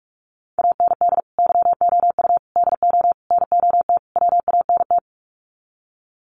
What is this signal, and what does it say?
Morse code, heard as a keyed tone, about 44 wpm, 700 Hz. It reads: AND YOU DO NOT WANT
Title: Sets of 5 Words Encore2 45wpm